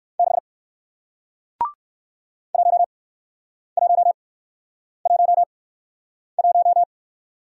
Morse code, heard as a keyed tone, about 55 wpm, spaced as for 9 wpm, 700 Hz. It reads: 6 1111
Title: Speed-Racing Single Letter-Number 45wpm